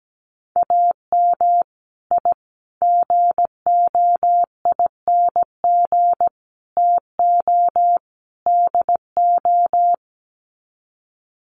Morse code, heard as a keyed tone, about 17 wpm, 700 Hz. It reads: AM I GOING TO DO
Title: Sets of 5 Words Encore 17wpm